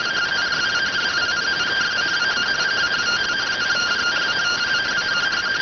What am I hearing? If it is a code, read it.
Two-tone teleprinter signal, about 170 Hz shift.